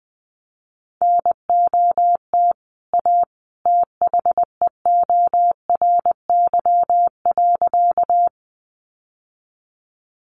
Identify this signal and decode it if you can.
Morse code, heard as a keyed tone, about 20 wpm, 700 Hz. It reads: NOT A THEORY.
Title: Sentences from Top 600 Words 20wpm